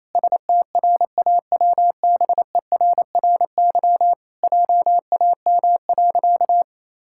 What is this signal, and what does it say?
Morse code, heard as a keyed tone, about 28 wpm, 700 Hz. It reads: STRAWBERRY JAM.
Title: Pangram Challenge v2 - 28wpm